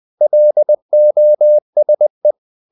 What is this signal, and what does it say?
Morse code, heard as a keyed tone, about 20 wpm, 600 Hz. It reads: LOSE
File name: the old man and the sea part 2 20wpm